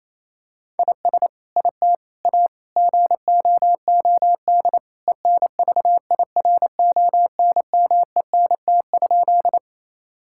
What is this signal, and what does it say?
Morse code, heard as a keyed tone, about 28 wpm, 700 Hz. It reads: IS IT A GOOD ENVIRONMENT?
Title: Sentences from Top 1000 Words - Review Part 5 - 28wpm